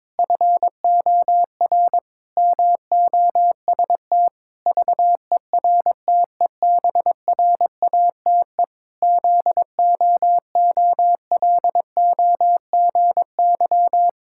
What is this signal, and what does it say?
Morse code, heard as a keyed tone, about 22 wpm, 700 Hz. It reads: FOR MOST VERTEBRATE ZOOLOGY
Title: Pangram Challenge v2 - 22wpm